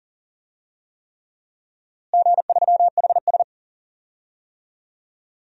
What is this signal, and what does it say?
Morse code, heard as a keyed tone, about 40 wpm, 700 Hz. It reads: G3HS
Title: DXCC-100 Call Signs - 4 Characters - Repeated 3x - 40wpm